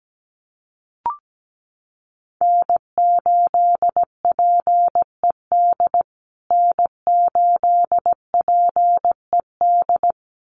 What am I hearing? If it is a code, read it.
Morse code, heard as a keyed tone, about 17 wpm, 700 Hz. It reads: N8PED N8PED
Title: USA Call Signs Repeated 2x 17wpm